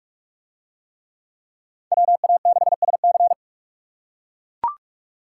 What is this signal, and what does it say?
Morse code, heard as a keyed tone, about 41 wpm, 700 Hz. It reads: WA6SC